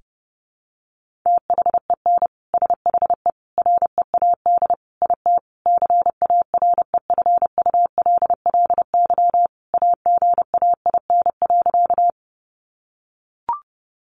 Morse code, heard as a keyed tone, about 30 wpm, 700 Hz. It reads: THEN SHE READ IT CAREFULLY AGAIN.
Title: Sentences from Top 1300 Words 30wpm